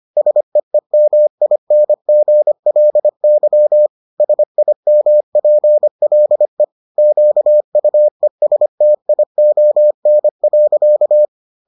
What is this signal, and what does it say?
Morse code, heard as a keyed tone, about 25 wpm, 600 Hz. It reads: SEEMINGLY SIMPLE QUESTION.